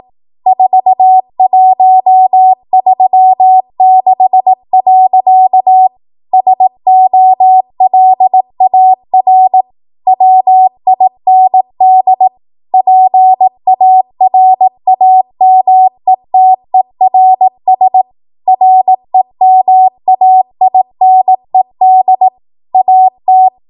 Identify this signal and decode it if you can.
Morse code, heard as a keyed tone, about 18 wpm, 750 Hz. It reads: 4136. SOLAR WIND PARAMETERS REMAINED AT
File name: Morse / 18WPM